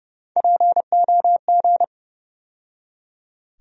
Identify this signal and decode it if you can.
Morse code, heard as a keyed tone, about 30 wpm, 700 Hz. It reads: POG
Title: USA Call Sign Postfixes - 3 Letters 30wpm